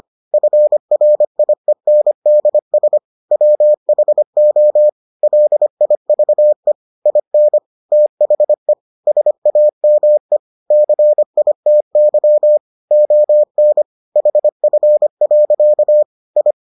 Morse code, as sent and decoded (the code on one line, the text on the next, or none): ..-. .-. .. -. -.. ... / .-- .... --- / .-.. .. ...- . / .. -. / - .... / ... .- -- . / -.-. .. - -.-- / --- -. / .... ..-. .-.-.- / ..
FRIENDS WHO LIVE IN THE SAME CITY ON HF. I